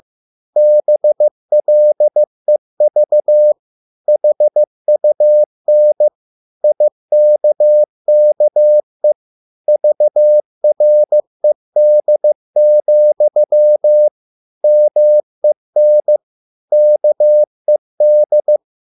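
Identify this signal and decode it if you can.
Morse code, heard as a keyed tone, about 15 wpm, 600 Hz. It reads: BLEV HUN IKKE VRED, MEN KED
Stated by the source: Morse code at a slow 15 wpm from CW med Gnister.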